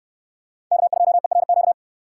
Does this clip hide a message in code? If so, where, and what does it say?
Yes: Morse code from 0.7 to 1.7 s, heard as a keyed tone, about 64 wpm, 700 Hz. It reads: K2ERQ